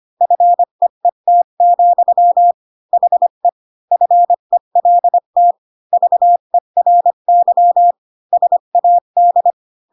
Morse code, heard as a keyed tone, about 25 wpm, 700 Hz. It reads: FEET, HE FELT VERY SAD
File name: Pinocchio / Pinocchio ch7 20 25 ik1hge